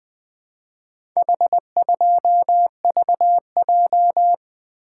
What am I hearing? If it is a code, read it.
Morse code, heard as a keyed tone, about 20 wpm, 700 Hz. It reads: H2VJ
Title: Four Letter-Number Combinations - 20wpm